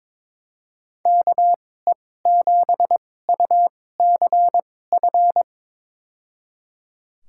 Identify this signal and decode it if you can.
Morse code, heard as a keyed tone, about 22 wpm, spaced as for 16 wpm, 700 Hz. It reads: KE7UCF
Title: USA Call Signs - 6 Characters - 2x Character Spacing - 22wpm